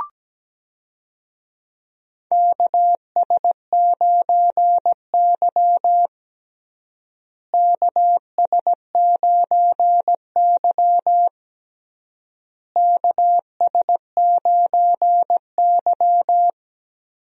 Morse code, heard as a keyed tone, about 17 wpm, 700 Hz. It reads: KS9Y KS9Y KS9Y
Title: USA Call Signs - 4 Characters - Repeated 3x - 17wpm